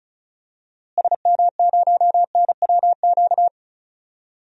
Morse code, heard as a keyed tone, about 36 wpm, 700 Hz. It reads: SM0NWQ
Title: DXCC-100 Call Signs - 6 Characters - Repeated 3x - 35wpm